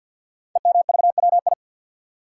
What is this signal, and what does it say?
Morse code, heard as a keyed tone, about 46 wpm, 700 Hz. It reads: EMVWI
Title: Code Groups - 5 Letters - 50wpm